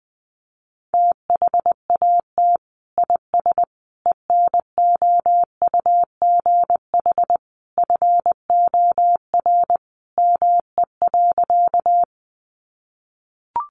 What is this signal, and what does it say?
Morse code, heard as a keyed tone, about 20 wpm, 700 Hz. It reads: THAT IS ENOUGH FOR ME.
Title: Sentences from Top 400 Words 20wpm